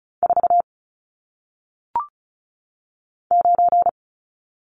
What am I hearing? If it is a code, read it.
Morse code, heard as a keyed tone, about 36 wpm, 700 Hz. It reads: 4 9
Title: Numbers - 1-Digit 35wpm